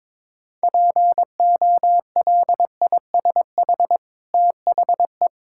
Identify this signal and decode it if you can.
Morse code, heard as a keyed tone, about 22 wpm, 700 Hz. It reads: POLISH THE
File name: practice / 22wpm / Easy Sentences with Hints - 22wpm